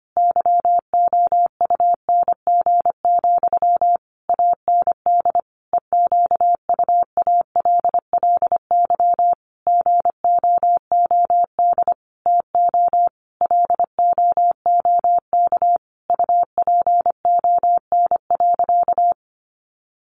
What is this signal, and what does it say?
Morse code, heard as a keyed tone, about 25 wpm, 700 Hz. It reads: YOUNG, AND EQUALLY GOOD TO LOOK UPON.
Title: Sentences from Top 1400 Words 25wpm